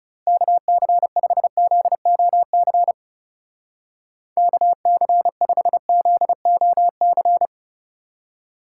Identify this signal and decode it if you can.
Morse code, heard as a keyed tone, about 36 wpm, 700 Hz. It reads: KC5ZOC KC5ZOC
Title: Speed Racing - USA Call Signs 30wpm